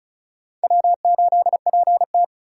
Morse code, heard as a keyed tone, about 34 wpm, 700 Hz. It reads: W8PT
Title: USA Call Signs - 4 Characters - 3x Character Spacing - 35wpm